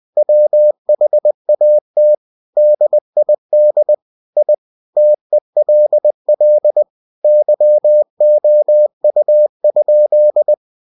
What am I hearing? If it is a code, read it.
Morse code, heard as a keyed tone, about 20 wpm, 600 Hz. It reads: WHAT DID I TELL YOU?
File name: cw-books / winnie-the-pooh / winnie-20wpm-page-0072